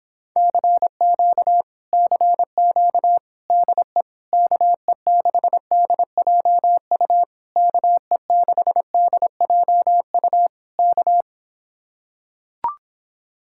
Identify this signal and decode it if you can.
Morse code, heard as a keyed tone, about 26 wpm, 700 Hz. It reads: CQ CQ DE KE6DJU KE6DJU K